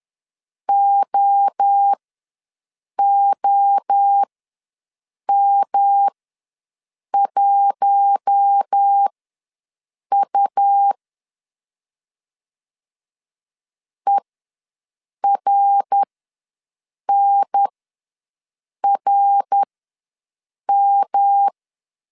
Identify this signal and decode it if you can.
Morse code, heard as a keyed tone, about 10 wpm, spaced as for 6 wpm, 800 Hz. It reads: OOM1U ERNRM
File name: LEZIONI / 5 LBVC / 5 L ws 10wpm 3636